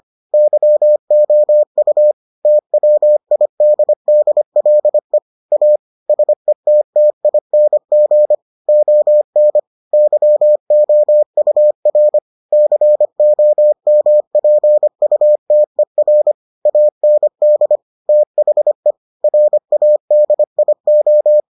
Morse code, heard as a keyed tone, about 25 wpm, 600 Hz. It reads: YOU TWIDDLE A SETTING ON YOUR COMPUTER AND THE RADIO